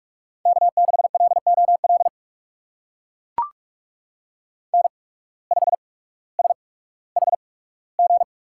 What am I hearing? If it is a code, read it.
Morse code, heard as a keyed tone, about 44 wpm, spaced as for 19 wpm, 700 Hz. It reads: K6LOL N5SHG
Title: USA Call Signs - 5 Characters - 8x Character Spacing - 45wpm